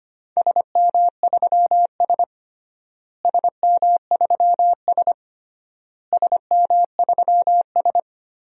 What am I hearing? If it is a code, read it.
Morse code, heard as a keyed tone, about 25 wpm, 700 Hz. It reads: SM3S SM3S SM3S
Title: DXCC-100 Call Signs - 4 Characters - Repeated 3x - 25wpm